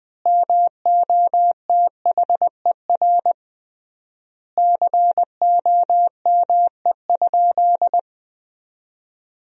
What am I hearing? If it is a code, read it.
Morse code, heard as a keyed tone, about 20 wpm, 700 Hz. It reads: MOTHER COME?
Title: 3x Word-Spacing - Sentences from Top 200 Words - 20wpm